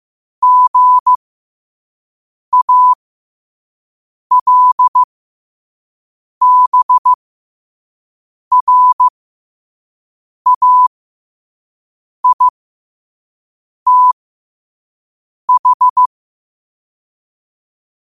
Morse code, heard as a keyed tone, about 15 wpm, spaced as for 5 wpm, 1000 Hz. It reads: GALBRAITH
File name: qotdfiles / 5wpm